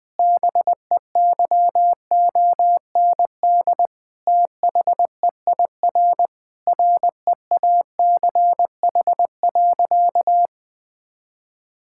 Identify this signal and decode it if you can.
Morse code, heard as a keyed tone, about 21 wpm, 700 Hz. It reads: BEYOND THEIR REACH.
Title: Sentences from Top 1000 Words - Review Part 6 - 20wpm